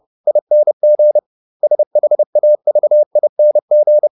Morse code, heard as a keyed tone, about 30 wpm, 600 Hz. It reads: ING SHAVING